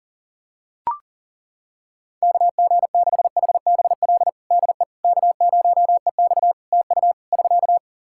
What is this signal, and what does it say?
Morse code, heard as a keyed tone, about 37 wpm, 700 Hz. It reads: KG6HBL DE K0EX TU <SK>
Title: K1USN SST QSO Practice 40wpm